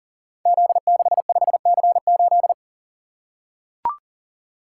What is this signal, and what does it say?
Morse code, heard as a keyed tone, about 40 wpm, 690 Hz. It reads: Z65C8
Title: Code Groups - 5 Letter:Numbers - 40wpm